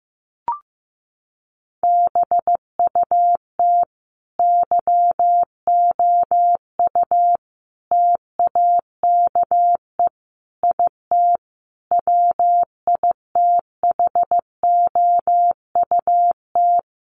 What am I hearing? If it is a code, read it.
Morse code, heard as a keyed tone, about 15 wpm, 700 Hz. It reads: BUT YOU TAKE IT WITHOUT
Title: Sentences from Top 1200 Words 15wpm